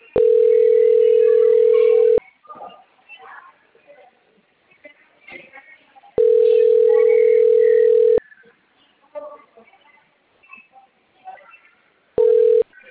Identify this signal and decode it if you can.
Ringback tone (North America), 440 + 480 Hz, with speech.
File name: out- - - - - .44641